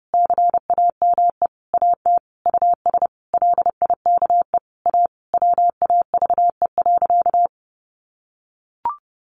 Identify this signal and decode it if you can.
Morse code, heard as a keyed tone, about 30 wpm, 700 Hz. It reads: CAME AT US LIKE A WAVE.
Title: Sentences from Top 1400 Words 30wpm